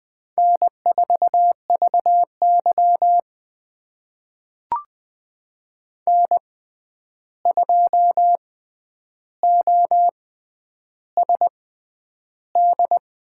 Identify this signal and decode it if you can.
Morse code, heard as a keyed tone, about 20 wpm, spaced as for 10 wpm, 700 Hz. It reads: N4VY N2OSD
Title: USA Call Signs - 6x Character Spacing - 20wpm